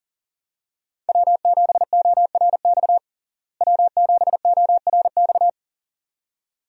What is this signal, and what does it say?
Morse code, heard as a keyed tone, about 40 wpm, 700 Hz. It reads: W7ORX W7ORX
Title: USA Call Signs - 5 Characters - Repeated 2x - 40wpm